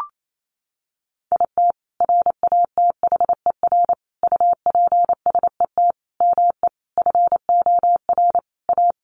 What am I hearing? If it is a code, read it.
Morse code, heard as a keyed tone, about 28 wpm, 690 Hz. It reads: IT RATHER UPSET ME FOR A
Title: Sentences from Top 900 Words 28wpm